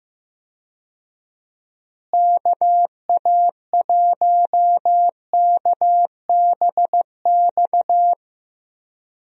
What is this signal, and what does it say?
Morse code, heard as a keyed tone, about 15 wpm, 700 Hz. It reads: KA1KBX
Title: USA Call Signs - 6 Characters - Repeated 2x - 15wpm